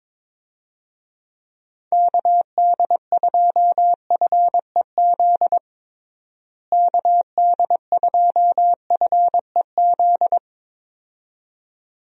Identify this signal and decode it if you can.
Morse code, heard as a keyed tone, about 22 wpm, 700 Hz. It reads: KD2FEZ KD2FEZ
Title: USA Call Signs - 6 Characters - Repeated 2x - 22wpm